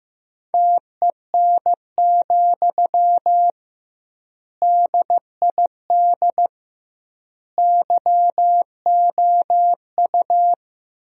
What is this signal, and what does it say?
Morse code, heard as a keyed tone, about 15 wpm, 700 Hz. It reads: TEN, DID YOU
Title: 2x Word-Spacing - Sentences from Top 400 Words - 15wpm